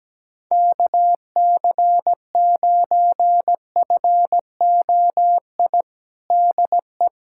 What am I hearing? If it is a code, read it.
Morse code, heard as a keyed tone, about 17 wpm, 700 Hz. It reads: KC9FOI DE
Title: USA Callsign DE Callsign - 6 Characters - 17wpm